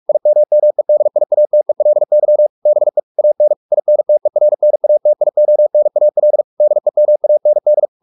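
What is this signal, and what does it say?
Morse code, heard as a keyed tone, about 44 wpm, 600 Hz. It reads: IMMEDIATELY BE AN INTERNATIONAL DEMAND
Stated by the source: Morse practice at 45 wpm, the novel Never by Ken Follett in English, from CW med Gnister.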